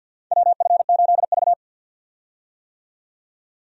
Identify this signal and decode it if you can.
Morse code, heard as a keyed tone, about 53 wpm, 700 Hz. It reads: WF8V
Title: USA Call Signs - 4 Characters - 50wpm